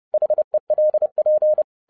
Morse code, heard as a keyed tone, about 30 wpm, 610 Hz. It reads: HELP
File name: 2011-07-mp3 / n60 BACK FROM A BIG ACCIDENT 30wpm